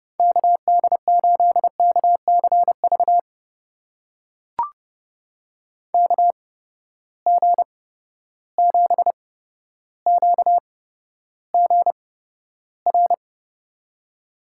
Morse code, heard as a keyed tone, about 30 wpm, spaced as for 13 wpm, 700 Hz. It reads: KD8KCV KG7QGR
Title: USA Call Signs - 6 Characters - 8x Character Spacing - 30wpm